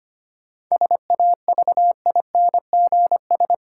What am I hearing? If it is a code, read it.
Morse code, heard as a keyed tone, about 25 wpm, 700 Hz. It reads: SAVINGS
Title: Sentences from Top 1200 Words - New Word Drill Alt - 25wpm